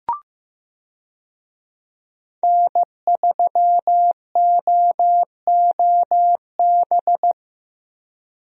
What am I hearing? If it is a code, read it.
Morse code, heard as a keyed tone, about 15 wpm, 700 Hz. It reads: N3OOB
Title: USA Call Signs - 5 Characters - Repeated 3x - 15wpm